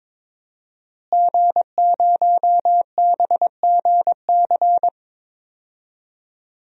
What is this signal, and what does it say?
Morse code, heard as a keyed tone, about 22 wpm, 700 Hz. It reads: G0BGC
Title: DXCC-100 Call Signs - 5 Characters - Repeated 2x - 22wpm